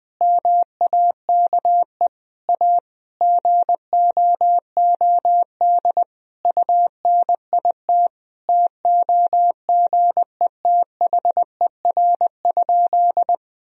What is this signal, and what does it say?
Morse code, heard as a keyed tone, about 20 wpm, 700 Hz. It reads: MAKE A GOOD UNIT TOGETHER?
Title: Sentences from Top 1000 Words - Review Part 2 - 20wpm